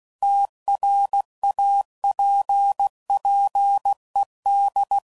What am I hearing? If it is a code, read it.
Morse code, heard as a keyed tone, about 16 wpm, 800 Hz. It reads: TRAPPED
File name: CW Code Groups @15WPM / QSOs / WA7QOX QRX